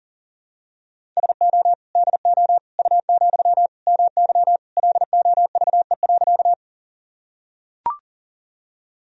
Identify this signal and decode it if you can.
Morse code, heard as a keyed tone, about 40 wpm, 700 Hz. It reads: SO DO U, MY LOVE.
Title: Sentences from Top 400 Words and Abbreviations 40wpm